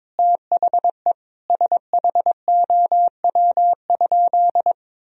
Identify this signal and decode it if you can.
Morse code, heard as a keyed tone, about 21 wpm, 700 Hz. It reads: THE SHOW?
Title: Sentences from Top 1000 Words - Review Part 2 - 22wpm